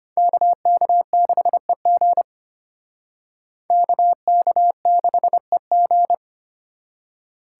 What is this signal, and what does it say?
Morse code, heard as a keyed tone, about 31 wpm, 700 Hz. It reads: KK6EG KK6EG